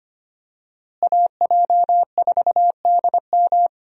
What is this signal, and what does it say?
Morse code, heard as a keyed tone, about 25 wpm, 700 Hz. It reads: AJ4DM
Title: USA Call Signs - 5 Characters - Repeated 2x - 25wpm